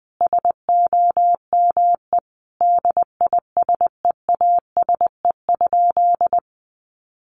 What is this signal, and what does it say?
Morse code, heard as a keyed tone, about 20 wpm, 700 Hz. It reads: SOME DISEASE?
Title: Sentences from Top 1500 Words 20wpm